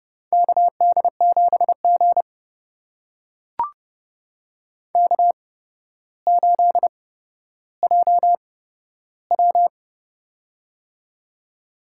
Morse code, heard as a keyed tone, about 30 wpm, spaced as for 13 wpm, 700 Hz. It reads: KD7G K8JW
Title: USA Call Signs - 8x Character Spacing - 30wpm